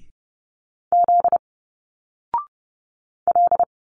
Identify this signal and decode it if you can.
Morse code, heard as a keyed tone, about 30 wpm, 700 Hz. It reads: Z L
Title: Single Letters 30wpm